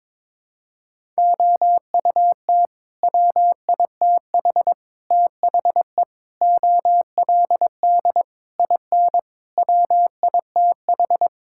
Morse code, heard as a keyed tone, about 22 wpm, 700 Hz. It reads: OUT WITH THE OLD IN WITH
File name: practice / 22wpm / Sentences with 8-Word Phrases - 22wpm